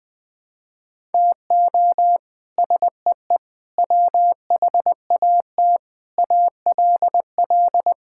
Morse code, heard as a keyed tone, about 20 wpm, 700 Hz. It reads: TO SEE WHAT ALL
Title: Sets of 8 Words - 20wpm